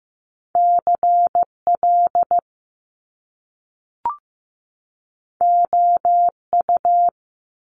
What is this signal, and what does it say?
Morse code, heard as a keyed tone, about 15 wpm, 700 Hz. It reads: CL OU
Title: Top Two Letter Combinations in Words 15wpm